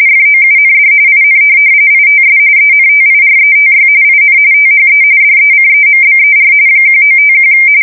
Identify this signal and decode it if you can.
RTTY, about 45.45 baud. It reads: FOUND IN ITS ENTIRETY IN THE DECEMBER 1951
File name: fsk170 / teletype-RTTY